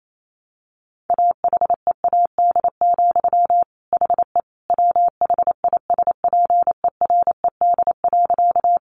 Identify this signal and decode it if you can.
Morse code, heard as a keyed tone, about 28 wpm, 700 Hz. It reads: AHEAD, HE WHISPERED.
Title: A Sound Of Thunder 28wpm